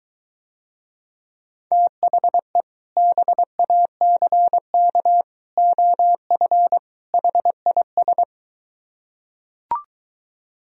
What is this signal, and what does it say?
Morse code, heard as a keyed tone, about 23 wpm, 700 Hz. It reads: THE BACK OF HIS